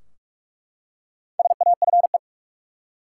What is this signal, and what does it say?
Morse code, heard as a keyed tone, about 47 wpm, 700 Hz. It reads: SAFE
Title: Four Letter Words - 45wpm